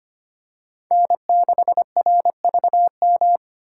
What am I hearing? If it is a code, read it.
Morse code, heard as a keyed tone, about 25 wpm, 700 Hz. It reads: N6RVM
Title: USA Call Signs - 2x Character Spacing - 25wpm